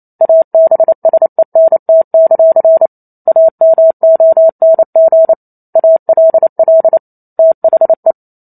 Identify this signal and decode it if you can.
Morse code, heard as a keyed tone, about 28 wpm, 650 Hz. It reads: ABSENT; AMONG ALL THE
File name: Chapter 2 - 28 WPM